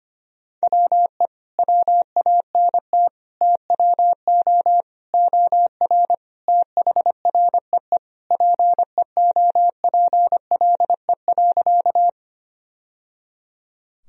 Morse code, heard as a keyed tone, about 25 wpm, 700 Hz. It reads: WE WANT TWO OR THREE PEOPLE.
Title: Sentences from Top 1000 Words - Review Part 1 - 25wpm